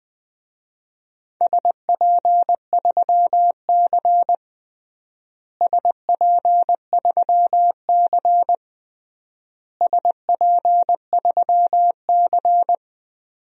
Morse code, heard as a keyed tone, about 20 wpm, 700 Hz. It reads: SP3C SP3C SP3C
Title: DXCC-100 Call Signs - 4 Characters - Repeated 3x - 20wpm